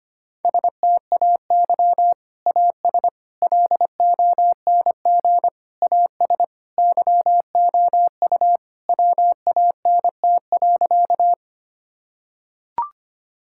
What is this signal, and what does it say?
Morse code, heard as a keyed tone, about 25 wpm, 700 Hz. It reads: STAY AS LONG AS YOU WANT.
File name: practice / 25wpm / Sentences with 3-Word Phrases - 25wpm